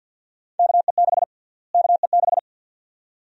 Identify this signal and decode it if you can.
Morse code, heard as a keyed tone, about 51 wpm, 700 Hz. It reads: KE6 KE6
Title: USA Call Sign Prefixes - Repeated 2x - 50wpm